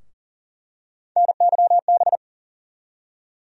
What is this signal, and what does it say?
Morse code, heard as a keyed tone, about 40 wpm, 700 Hz. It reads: NYB
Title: USA Call Sign Postfixes - 3 Letters 40wpm